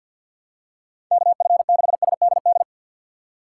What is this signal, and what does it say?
Morse code, heard as a keyed tone, about 52 wpm, 700 Hz. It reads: KF6SDD
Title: USA Call Signs - 6 Characters - 8x Character Spacing - 50wpm